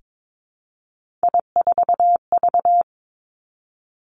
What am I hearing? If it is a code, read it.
Morse code, heard as a keyed tone, about 22 wpm, 700 Hz. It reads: I4V
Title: Three Challenging Letter-Number Combinations 22wpm